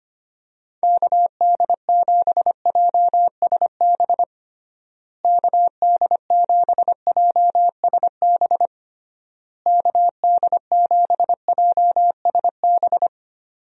Morse code, heard as a keyed tone, about 25 wpm, 700 Hz. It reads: KD7JSB KD7JSB KD7JSB
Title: USA Call Signs - 6 Characters - Repeated 3x - 25wpm